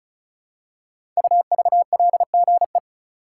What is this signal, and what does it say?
Morse code, heard as a keyed tone, about 34 wpm, 700 Hz. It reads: UVLGE
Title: Code Groups - 5 Letter:Numbers - 35wpm